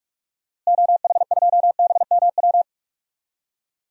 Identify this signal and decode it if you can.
Morse code, heard as a keyed tone, about 45 wpm, 700 Hz. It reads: OH2BMW
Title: DXCC-100 Call Signs - 6 Characters - Repeated 2x - 45wpm